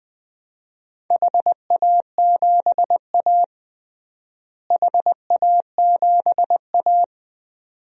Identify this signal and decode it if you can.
Morse code, heard as a keyed tone, about 20 wpm, 700 Hz. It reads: HA7A HA7A
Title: DXCC-100 Call Signs - 4 Characters - Repeated 2x - 20wpm